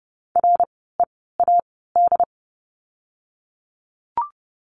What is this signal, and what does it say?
Morse code, heard as a keyed tone, about 30 wpm, spaced as for 17 wpm, 700 Hz. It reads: READ
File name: Lesson 16f - Letter c - Words 30wpm